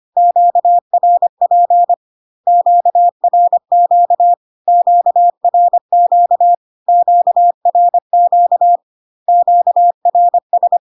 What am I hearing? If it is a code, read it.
Morse code, heard as a keyed tone, about 25 wpm, 700 Hz. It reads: QRP QRQ QRQ QRQ QRS
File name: Q-codes most common repeated 3 times 25 25wpm